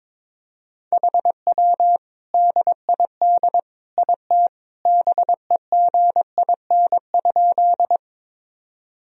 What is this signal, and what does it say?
Morse code, heard as a keyed tone, about 22 wpm, 700 Hz. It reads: HW DID IT BEGIN?
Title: Sentences from Top 300 Words and Abbreviations 22wpm